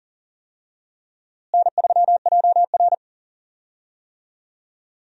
Morse code, heard as a keyed tone, about 38 wpm, 700 Hz. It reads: N3JR
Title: USA Call Signs - 4 Characters - Repeated 3x - 40wpm